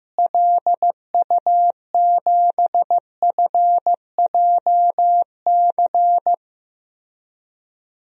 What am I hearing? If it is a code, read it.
Morse code, heard as a keyed tone, about 15 wpm, 700 Hz. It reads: LU7FJC
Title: DXCC-100 Call Signs - 6 Characters - Repeated 3x - 15wpm